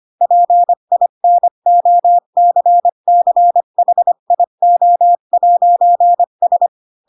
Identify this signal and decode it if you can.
Morse code, heard as a keyed tone, about 25 wpm, 700 Hz. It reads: PINOCCHIO'S
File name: Pinocchio / Pinocchio ch3 20 25 ik1hge